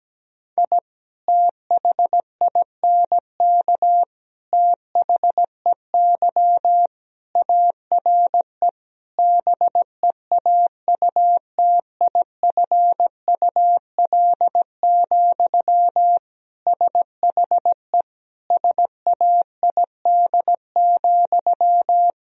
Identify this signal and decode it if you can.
Morse code, heard as a keyed tone, about 17 wpm, 700 Hz. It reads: I THINK THEY ARE BEAUTIFUL, SHE SAID,
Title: Sentences from Top 1000 Words - Review Part 4 - 17wpm